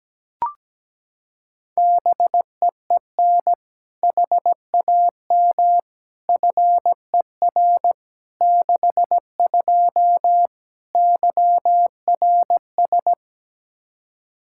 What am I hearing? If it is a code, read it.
Morse code, heard as a keyed tone, about 17 wpm, 700 Hz. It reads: BEEN HAM FER 62 YRS
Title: K1USN SST QSO Practice 17wpm